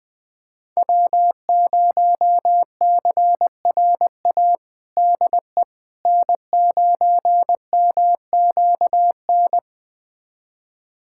Morse code, heard as a keyed tone, about 21 wpm, 700 Hz. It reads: W0CRA DE N9MQN
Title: USA Callsign DE Callsign - 5 Characters - 20wpm